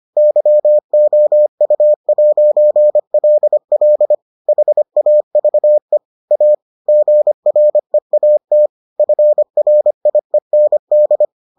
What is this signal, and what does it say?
Morse code, heard as a keyed tone, about 25 wpm, 600 Hz. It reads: YOU'LL HAVE A GREAT FRIEND